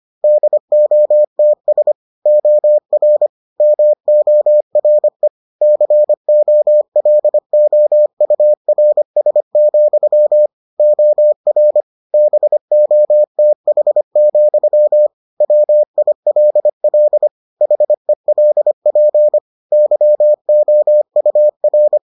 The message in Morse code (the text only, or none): DOTS OR MORE COLOURS, OR BOTH, WILL HELP YOUR